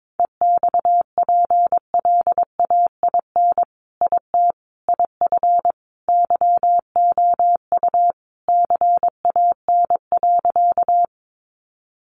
Morse code, heard as a keyed tone, about 22 wpm, 700 Hz. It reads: EXPLAIN IT IF YOU CAN.
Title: Sentences from Top 1100 Words 22wpm